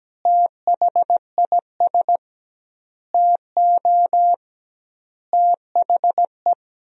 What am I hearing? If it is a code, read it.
Morse code, heard as a keyed tone, about 17 wpm, 700 Hz. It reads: THIS TO THE
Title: 2x Word-Spacing - Sentences from Top 200 Words - 17wpm